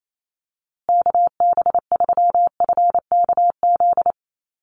Morse code, heard as a keyed tone, about 28 wpm, 700 Hz. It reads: KB3FKZ
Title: USA Call Signs Repeated 3x 28wpm